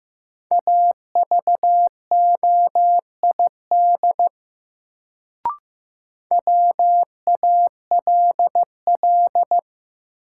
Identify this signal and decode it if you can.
Morse code, heard as a keyed tone, about 15 wpm, 700 Hz. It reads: AVOID WALL
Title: Sentences from Top 1000 Words - Review Part 6 - New Word Drill - 15wpm